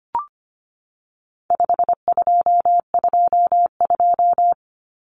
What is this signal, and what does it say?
Morse code, heard as a keyed tone, about 25 wpm, 700 Hz. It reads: 5222
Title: Numbers - 4-Digit 25wpm